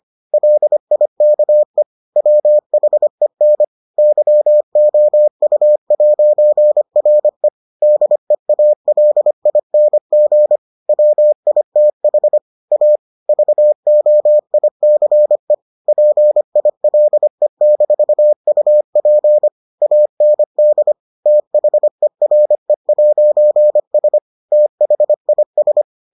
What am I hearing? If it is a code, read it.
Morse code, heard as a keyed tone, about 25 wpm, 600 Hz. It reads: LIKE WHEN YOU'RE DEALING WITH A VOICE PILE-UP AND THERE'S THIS